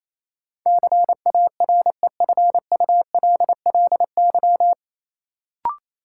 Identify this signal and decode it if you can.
Morse code, heard as a keyed tone, about 28 wpm, 700 Hz. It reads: CAREFULLY
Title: Sentences from Top 1300 Words - New Word Drill Alt - 28wpm